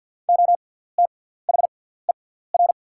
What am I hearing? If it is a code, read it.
Morse code, heard as a keyed tone, about 48 wpm, spaced as for 17 wpm, 700 Hz. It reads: OTHER
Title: 6x Character Spacing - Top 100 Words - 50wpm